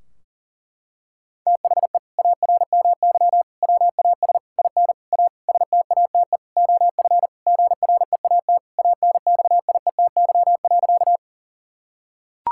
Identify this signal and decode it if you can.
Morse code, heard as a keyed tone, about 40 wpm, 700 Hz. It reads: THE ARMY WAS IN A STATE OF GREAT ANXIETY.
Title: Sentences from Top 1600 Words 40wpm